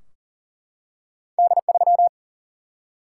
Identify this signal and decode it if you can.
Morse code, heard as a keyed tone, about 40 wpm, 700 Hz. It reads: D 3
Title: Single-Letter Single-Number 40wpm